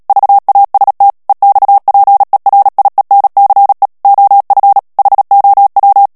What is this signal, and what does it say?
Morse code, heard as a keyed tone, about 37 wpm, 800 Hz. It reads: VAST EXPERIENCE OF HOW
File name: morse mp3 / ANGLAIS texte clair 2 vitesse 35 mots minute